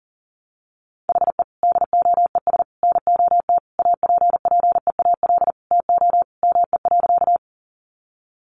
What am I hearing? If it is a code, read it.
Morse code, heard as a keyed tone, about 37 wpm, 700 Hz. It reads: HE DOES NOT APPEAL TO ME.
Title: Sentences from Top 1300 Words 40wpm